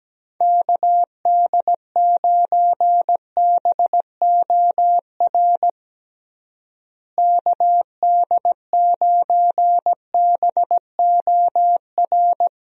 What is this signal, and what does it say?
Morse code, heard as a keyed tone, about 17 wpm, 700 Hz. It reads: KD9BOR KD9BOR
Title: USA Call Signs - 6 Characters - Repeated 3x - 17wpm